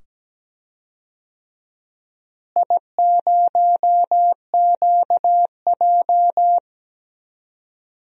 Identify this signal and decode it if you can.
Morse code, heard as a keyed tone, about 17 wpm, 700 Hz. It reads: I0QJ
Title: DXCC-100 Call Signs - 4 Characters - Repeated 2x - 17wpm